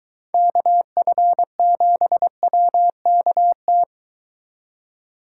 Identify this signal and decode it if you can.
Morse code, heard as a keyed tone, about 22 wpm, 700 Hz. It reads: KF7WKT